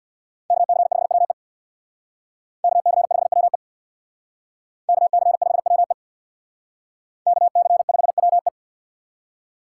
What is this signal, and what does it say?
Morse code, heard as a keyed tone, about 62 wpm, 700 Hz. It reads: KC5WE KC5WE KC5WE KC5WE
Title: Speed Racing - USA 2x2 Call Signs 50wpm